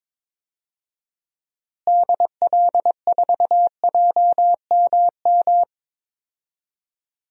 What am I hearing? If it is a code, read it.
Morse code, heard as a keyed tone, about 22 wpm, 700 Hz. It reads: DL4JMM
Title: DXCC-100 Call Signs - 6 Characters - Repeated 2x - 22wpm